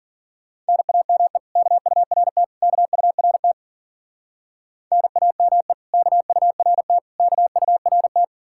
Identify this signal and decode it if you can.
Morse code, heard as a keyed tone, about 48 wpm, 690 Hz. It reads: NAME KURT KURT NAME KURT KURT
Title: Speed-Racing Common QSO Phrases Encore - 40wpm